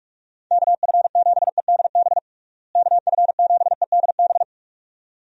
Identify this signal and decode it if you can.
Morse code, heard as a keyed tone, about 46 wpm, 700 Hz. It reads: KF7EDB KF7EDB
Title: USA Call Signs - 6 Characters - Repeated 2x - 45wpm